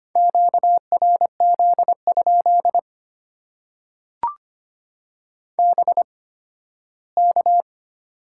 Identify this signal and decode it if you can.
Morse code, heard as a keyed tone, about 25 wpm, spaced as for 15 wpm, 700 Hz. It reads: QRZ? BK
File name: Top 90 QSO Elements - 8x Character Spacing - 25wpm